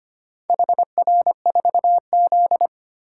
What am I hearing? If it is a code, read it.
Morse code, heard as a keyed tone, about 25 wpm, 700 Hz. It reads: HR4Z
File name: practice / 25wpm / Four Letter-Number Combinations - 25wpm